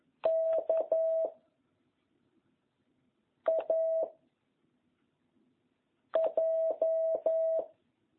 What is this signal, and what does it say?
Morse code, heard as a keyed tone, about 11 wpm, 650 Hz. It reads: K A J